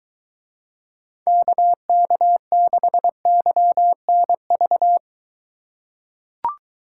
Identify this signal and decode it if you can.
Morse code, heard as a keyed tone, about 23 wpm, 700 Hz. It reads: KK6YNV